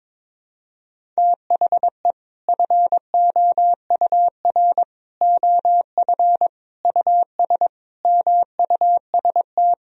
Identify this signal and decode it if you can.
Morse code, heard as a keyed tone, about 22 wpm, 700 Hz. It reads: THE FOUR OF US MUST
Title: Sentences from Top 1000 Words - Review Part 1 - 22wpm